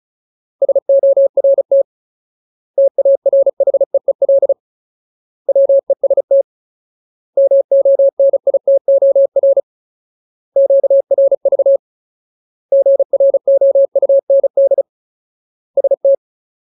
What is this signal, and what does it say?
Morse code, heard as a keyed tone, about 34 wpm, 550 Hz. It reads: SORT TARHEEL WEST MONITOR QRV GROUND ST
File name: Ultimate Word List / 35WPM Ultimate